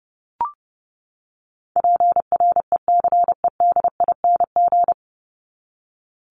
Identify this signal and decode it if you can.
Morse code, heard as a keyed tone, about 30 wpm, 700 Hz. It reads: PRECEDING
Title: ING Suffix - Top 100 Words with Base Words 30wpm